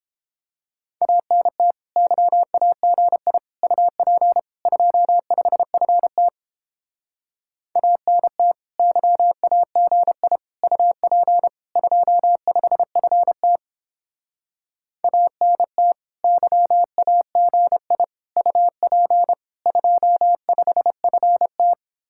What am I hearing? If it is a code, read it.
Morse code, heard as a keyed tone, about 29 wpm, 700 Hz. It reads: ANT YAGI UP 25FT ANT YAGI UP 25FT ANT YAGI UP 25FT